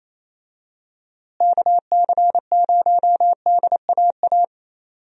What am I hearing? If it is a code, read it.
Morse code, heard as a keyed tone, about 28 wpm, 700 Hz. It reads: KC0DAA
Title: USA Call Signs - 6 Characters - Repeated 3x - 28wpm